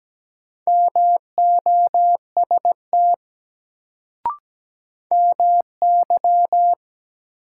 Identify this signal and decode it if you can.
Morse code, heard as a keyed tone, about 17 wpm, 700 Hz. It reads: MOST MY
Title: Sentences from Top 200 Words - New Word Drill - 17wpm